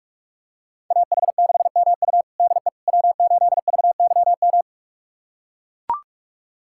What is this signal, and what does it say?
Morse code, heard as a keyed tone, about 45 wpm, 700 Hz. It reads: AH6MU DE W8VYM
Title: USA Callsign DE Callsign - 5 Characters - 45wpm